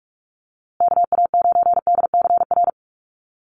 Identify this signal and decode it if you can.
Morse code, heard as a keyed tone, about 44 wpm, 700 Hz. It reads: KA9DCR
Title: USA Call Signs Repeated 2x 45wpm